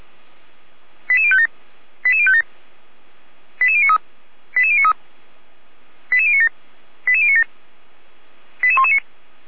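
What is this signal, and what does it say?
Five-tone selective call (ZVEI), heard as quick runs of five tones, tones of about 70 ms. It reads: 88057 88057 88093 88093 88097 88097 88119
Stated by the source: ZVEI five-tone call received by SDR, from feuerwehr-orlamuende.org fire brigade audio archive.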